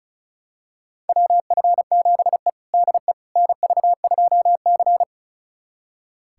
Morse code, heard as a keyed tone, about 34 wpm, 700 Hz. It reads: WF7E DE NV2C
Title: USA Callsign DE Callsign - 4 Characters - 35wpm